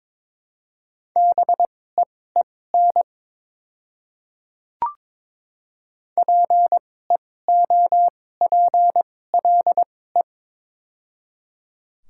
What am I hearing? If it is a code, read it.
Morse code, heard as a keyed tone, about 22 wpm, spaced as for 16 wpm, 700 Hz. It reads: BEEN PEOPLE
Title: 2x Character Spacing - Top 100 Words - 22wpm